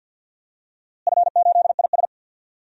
Morse code, heard as a keyed tone, about 51 wpm, 700 Hz. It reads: F8IS
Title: DXCC-100 Call Signs - 4 Characters - Repeated 3x - 50wpm